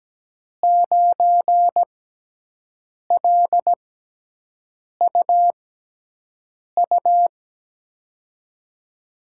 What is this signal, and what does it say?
Morse code, heard as a keyed tone, about 17 wpm, spaced as for 6 wpm, 700 Hz. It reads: 9LUU
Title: USA Call Signs - 5 Characters - 6x Character Spacing - 17wpm